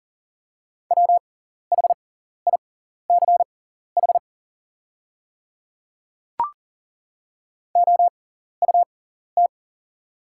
Morse code, heard as a keyed tone, about 40 wpm, spaced as for 14 wpm, 690 Hz. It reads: WHICH OUT
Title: 6x Character Spacing - Top 300 Words - 40wpm